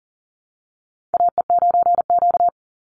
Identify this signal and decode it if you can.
Morse code, heard as a keyed tone, about 40 wpm, 700 Hz. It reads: AE9Q
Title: NA SOTA Chaser Callsigns 40wpm